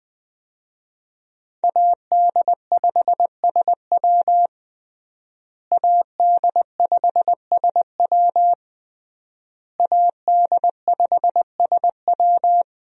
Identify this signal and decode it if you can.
Morse code, heard as a keyed tone, about 21 wpm, 700 Hz. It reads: AD5SW AD5SW AD5SW
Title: USA Call Signs - 5 Characters - Repeated 3x - 20wpm